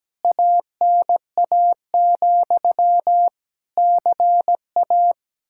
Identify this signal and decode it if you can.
Morse code, heard as a keyed tone, about 17 wpm, 700 Hz. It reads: ANA, CA
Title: K1USN SST QSO Practice 17wpm